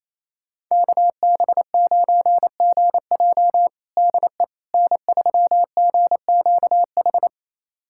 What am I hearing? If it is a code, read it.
Morse code, heard as a keyed tone, about 28 wpm, 700 Hz. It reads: KB9GJ DE N3GQH
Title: USA Callsign DE Callsign - 5 Characters - 28wpm